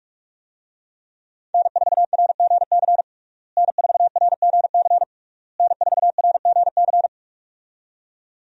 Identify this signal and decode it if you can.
Morse code, heard as a keyed tone, about 41 wpm, 700 Hz. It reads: N4RGC N4RGC N4RGC
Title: USA Call Signs - 5 Characters - Repeated 3x - 45wpm